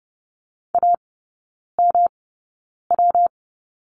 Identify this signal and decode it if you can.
Morse code, heard as a keyed tone, about 30 wpm, spaced as for 9 wpm, 700 Hz. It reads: AMW
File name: Lesson 19e - Letter w - Three Characters 30wpm